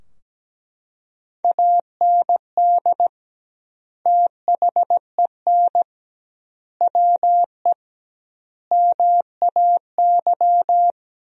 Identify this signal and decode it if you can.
Morse code, heard as a keyed tone, about 17 wpm, 700 Hz. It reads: AND THEN WE MAY
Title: 2x Word-Spacing - Sentences from Top 300 Words - 17wpm